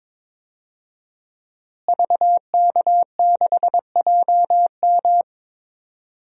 Morse code, heard as a keyed tone, about 22 wpm, 700 Hz. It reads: VK6JM
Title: DXCC-100 Call Signs - 5 Characters - Repeated 2x - 22wpm